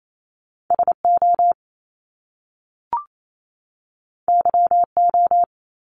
Morse code, heard as a keyed tone, about 28 wpm, 700 Hz. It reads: SO YO
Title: Top Two Letter Combinations in Words 28wpm